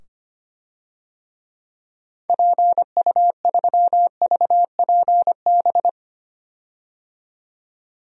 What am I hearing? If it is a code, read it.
Morse code, heard as a keyed tone, about 25 wpm, 690 Hz. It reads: PU3VPB
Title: DXCC-100 Call Signs - 6 Characters - 25wpm